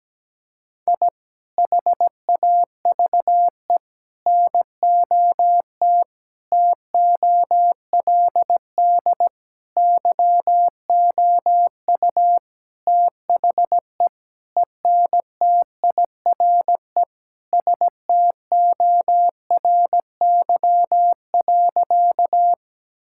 Morse code, heard as a keyed tone, about 17 wpm, 700 Hz. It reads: I HAVE NOT TOLD YOU THE ENTIRE STORY.
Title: Sentences from Top 1000 Words - Review Part 5 - 17wpm